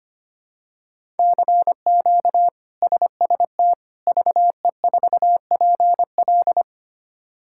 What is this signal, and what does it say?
Morse code, heard as a keyed tone, about 25 wpm, 700 Hz. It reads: CQ SST VE4PL
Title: K1USN SST QSO Practice - 25wpm